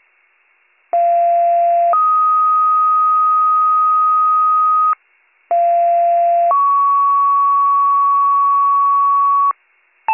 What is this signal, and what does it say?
Two-tone page, heard as a one-second tone, then a three-second tone, about 690 then 1250 Hz, about 690 then 1120 Hz. It is heard from 0.9 to 9.5 s.